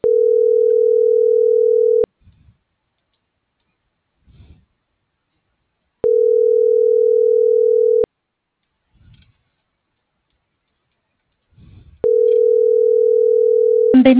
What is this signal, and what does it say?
Ringback tone (North America), 440 + 480 Hz.